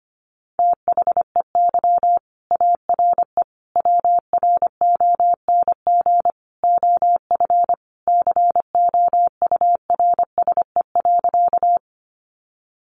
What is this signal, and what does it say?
Morse code, heard as a keyed tone, about 25 wpm, 700 Hz. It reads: THEY ARE WRONG OF COURSE.
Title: Sentences from Top 900 Words 25wpm